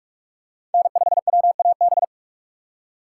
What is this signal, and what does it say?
Morse code, heard as a keyed tone, about 44 wpm, 700 Hz. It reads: N5WAB
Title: USA Call Signs - 6x Character Spacing - 45wpm